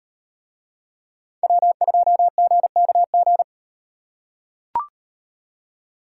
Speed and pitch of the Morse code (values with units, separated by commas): 36 wpm, 700 Hz